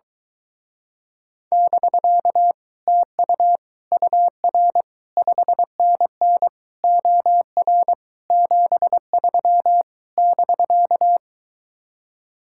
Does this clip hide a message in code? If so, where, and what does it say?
Yes: Morse code from 1.5 to 11.2 s, heard as a keyed tone, about 22 wpm, 700 Hz. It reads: <BK> TU UR 5NN OR 73 <BK>